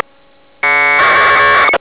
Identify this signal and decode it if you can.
Packet radio (AX.25, as in APRS).